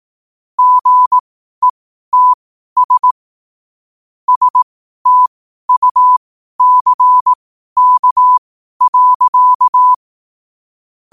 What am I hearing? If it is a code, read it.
Morse code, heard as a keyed tone, about 18 wpm, spaced as for 12 wpm, 1000 Hz. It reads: GETS STUCK.
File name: qotdfiles / 12wpm